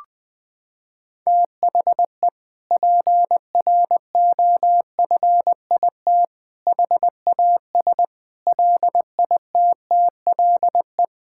Morse code, heard as a keyed tone, about 20 wpm, 700 Hz. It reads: THE PROFIT HAS LITTLE
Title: Sentences from Top 1000 Words - Review Part 5 - 20wpm